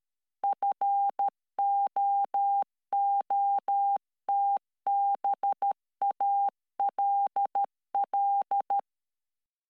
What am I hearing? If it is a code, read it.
Morse code, heard as a keyed tone, about 13 wpm, 800 Hz. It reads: FOOTBALL